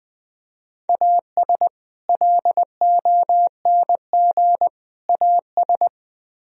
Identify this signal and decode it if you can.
Morse code, heard as a keyed tone, about 20 wpm, 700 Hz. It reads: AS LONG AS
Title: Speed-Racing - Sets of 3 Words 17wpm